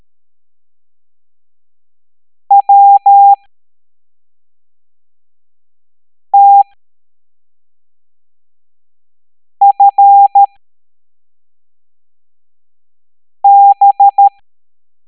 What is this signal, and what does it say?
Morse code, heard as a keyed tone, about 13 wpm, spaced as for 3 wpm, 800 Hz. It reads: WTFB